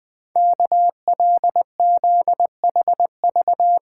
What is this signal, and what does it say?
Morse code, heard as a keyed tone, about 20 wpm, 700 Hz. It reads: KLZHV
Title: Code Groups - 5 Letters - 20wpm